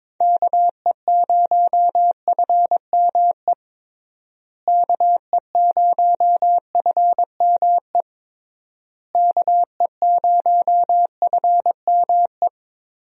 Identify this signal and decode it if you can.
Morse code, heard as a keyed tone, about 22 wpm, 700 Hz. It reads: KE0FME KE0FME KE0FME
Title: USA Call Signs - 6 Characters - Repeated 3x - 22wpm